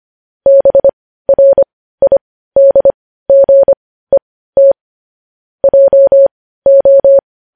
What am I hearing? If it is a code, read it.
Morse code, heard as a keyed tone, about 25 wpm, spaced as for 15 wpm, 550 Hz. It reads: BRIDGET JO
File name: podcast / News headlines-1x-25x15